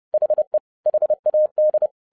Morse code, heard as a keyed tone, about 30 wpm, 610 Hz. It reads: HE HAD